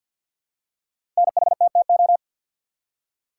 Morse code, heard as a keyed tone, about 53 wpm, 700 Hz. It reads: NHTTO
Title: Code Groups - 5 Letters - 50wpm